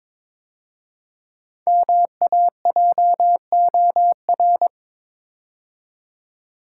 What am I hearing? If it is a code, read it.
Morse code, heard as a keyed tone, about 22 wpm, 700 Hz. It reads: MAJOR
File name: 5 Letter Words - First 3 Letters Given - 22wpm